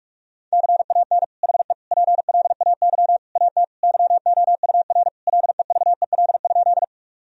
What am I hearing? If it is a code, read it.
Morse code, heard as a keyed tone, about 46 wpm, 700 Hz. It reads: CAN HE PLAY AT YOUR LEVEL?
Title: Sentences from Top 1000 Words - Review Part 4 - 45wpm